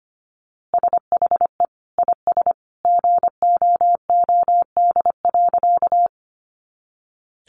Morse code, heard as a keyed tone, about 25 wpm, 700 Hz. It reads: SHE IS GOOD.
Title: Easy Phrases 25wpm